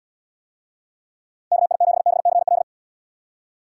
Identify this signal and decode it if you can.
Morse code, heard as a keyed tone, about 78 wpm, 700 Hz. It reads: KI7BCL